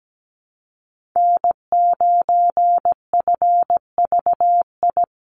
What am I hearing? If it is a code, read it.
Morse code, heard as a keyed tone, about 17 wpm, 700 Hz. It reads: N9FVI
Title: USA Call Signs Repeated 3x 17wpm